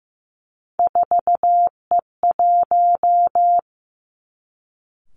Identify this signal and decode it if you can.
Morse code, heard as a keyed tone, about 15 wpm, 700 Hz. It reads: 4E1
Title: Three Challenging Letter-Number Combinations 15wpm